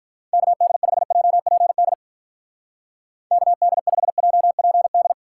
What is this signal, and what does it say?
Morse code, heard as a keyed tone, about 42 wpm, 700 Hz. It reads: KD5JPD KD5JPD
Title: Speed Racing - USA 2x3 Call Signs 35wpm